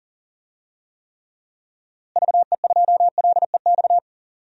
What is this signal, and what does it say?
Morse code, heard as a keyed tone, about 37 wpm, 700 Hz. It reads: VE2LEX
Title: DXCC-100 Call Signs - 6 Characters - Repeated 2x - 40wpm